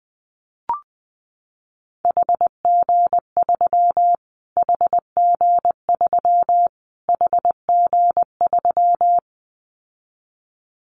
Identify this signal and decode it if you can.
Morse code, heard as a keyed tone, about 19 wpm, 700 Hz. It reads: HG3 HG3 HG3
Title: Callsign Prefixes for Top 25 DXCC - Repeated 3x - 20wpm